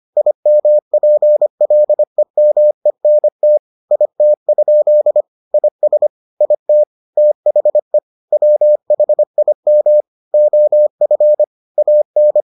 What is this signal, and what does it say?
Morse code, heard as a keyed tone, about 25 wpm, 600 Hz. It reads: IMPLEMENT IT? IS IT THE WHIM OF AN